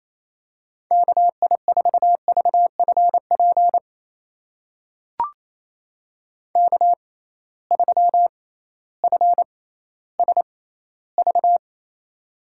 Morse code, heard as a keyed tone, about 28 wpm, spaced as for 15 wpm, 700 Hz. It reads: KI4VFP K3FSV
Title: USA Call Signs - 6x Character Spacing - 28wpm